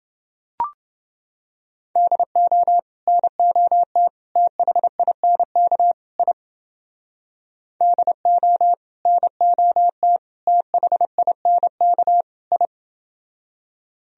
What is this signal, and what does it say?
Morse code, heard as a keyed tone, about 29 wpm, 700 Hz. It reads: DO NOT THINK I DO NOT THINK I